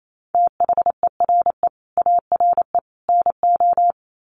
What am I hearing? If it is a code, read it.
Morse code, heard as a keyed tone, about 28 wpm, 700 Hz. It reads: THERE ARE NO
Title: Sets of 3 Words 28wpm